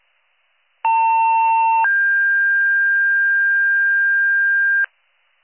Two-tone page, heard as a one-second tone, then a three-second tone, about 900 then 1600 Hz, with noise. It is heard from 0.8 to 4.9 s.